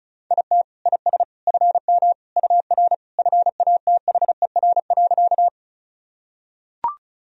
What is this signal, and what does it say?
Morse code, heard as a keyed tone, about 36 wpm, 700 Hz. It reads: IT IS FM UR FATHER.
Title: Sentences from Top 300 Words and Abbreviations 35wpm